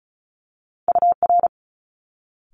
Morse code, heard as a keyed tone, about 36 wpm, 700 Hz. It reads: UR
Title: Multi-letter Phonemes Quick Fire 35wpm